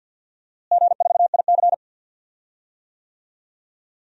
Morse code, heard as a keyed tone, about 53 wpm, 700 Hz. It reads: G4IC
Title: DXCC-100 Call Signs - 4 Characters - Repeated 2x - 50wpm